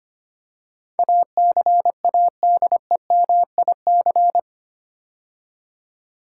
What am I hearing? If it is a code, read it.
Morse code, heard as a keyed tone, about 25 wpm, 700 Hz. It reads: ACADEMIC
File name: 8 Letter Words - First 3 Letters Given - 25wpm